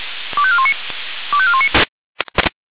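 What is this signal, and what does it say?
Five-tone selective call (ZVEI), heard as quick runs of five tones, tones of about 70 ms. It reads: 25419 25419
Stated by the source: Five-tone ZVEI selective call received on an SDR.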